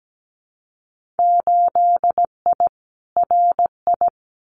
Morse code, heard as a keyed tone, about 17 wpm, 700 Hz. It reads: 8I RI
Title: Winter Field Day Exchanges Encore 17wpm